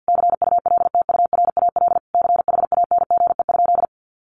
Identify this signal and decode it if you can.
Morse code, heard as a keyed tone, about 50 wpm, 700 Hz. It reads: CULTURAL CHANGE?
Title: Sentences from Top 1500 Words 50wpm